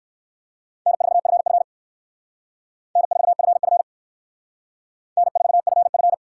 Morse code, heard as a keyed tone, about 62 wpm, 700 Hz. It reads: N4FF N4FF N4FF